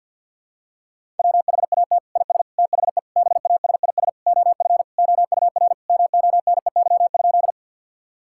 Morse code, heard as a keyed tone, about 46 wpm, 700 Hz. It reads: WHAT IS THE BASIS OF OUR MONEY?
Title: Sentences from Top 1000 Words - Review Part 5 - 50wpm